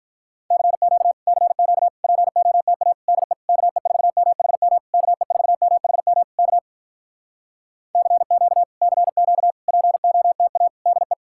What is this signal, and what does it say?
Morse code, heard as a keyed tone, about 48 wpm, 700 Hz. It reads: CQ CQ POTA DE KE4MHM KE4MHM K CQ CQ POTA DE